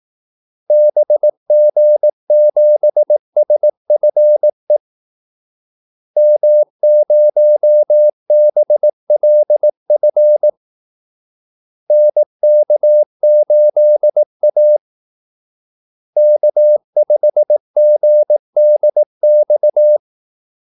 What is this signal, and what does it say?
Morse code, heard as a keyed tone, about 18 wpm, 600 Hz. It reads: BG7SFE M0BLF NK8A K5GDX
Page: Kaldesignaler 18 wpm